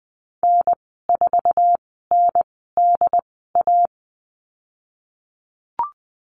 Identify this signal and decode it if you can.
Morse code, heard as a keyed tone, about 20 wpm, spaced as for 15 wpm, 700 Hz. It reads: N4NDA
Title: 069 Lesson 12g - Letter d - Call Signs 20wpm